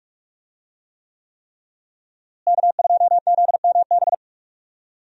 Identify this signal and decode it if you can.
Morse code, heard as a keyed tone, about 46 wpm, 700 Hz. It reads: K2ZMB
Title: USA Call Signs - 5 Characters - Repeated 3x - 45wpm